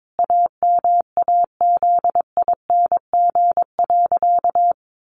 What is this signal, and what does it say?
Morse code, heard as a keyed tone, about 22 wpm, 700 Hz. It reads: AMAZING.
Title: Sentences from Top 1500 Words 22wpm